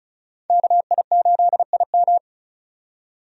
Morse code, heard as a keyed tone, about 36 wpm, 700 Hz. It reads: KI8IM
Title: USA Call Signs - 5 Characters - 35wpm